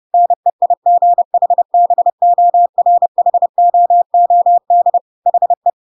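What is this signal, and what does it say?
Morse code, heard as a keyed tone, about 30 wpm, 700 Hz. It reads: NEIGHBORHOOD HE
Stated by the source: Chapter 2 of The Adventures of Pinocchio read as a book in Morse code (CW) at 30 wpm, by ik1hge.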